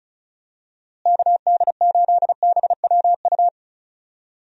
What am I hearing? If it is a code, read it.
Morse code, heard as a keyed tone, about 36 wpm, 700 Hz. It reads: KD8BWU
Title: USA Call Signs - 6 Characters - Repeated 2x - 35wpm